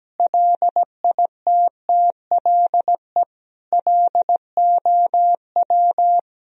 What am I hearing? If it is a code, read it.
Morse code, heard as a keyed tone, about 17 wpm, 700 Hz. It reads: LITTLE LOW
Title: Sentences from Top 1000 Words - Review Part 3 - 17wpm